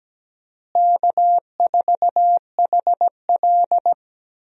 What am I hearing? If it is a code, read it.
Morse code, heard as a keyed tone, about 17 wpm, 700 Hz. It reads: K4HL
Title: USA Call Signs - 4 Characters - Repeated 3x - 17wpm